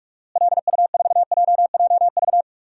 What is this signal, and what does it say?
Morse code, heard as a keyed tone, about 46 wpm, 700 Hz. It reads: LU4JJV
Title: DXCC-100 Call Signs - 6 Characters - Repeated 3x - 45wpm